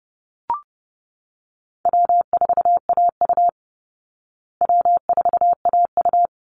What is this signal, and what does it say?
Morse code, heard as a keyed tone, about 30 wpm, 700 Hz. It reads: W4AU W4AU
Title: Easy Call Signs Repeated 2x - 30wpm